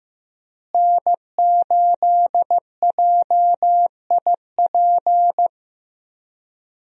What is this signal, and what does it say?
Morse code, heard as a keyed tone, about 16 wpm, 700 Hz. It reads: N8JIP
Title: USA Call Signs - 5 Characters - Repeated 2x - 15wpm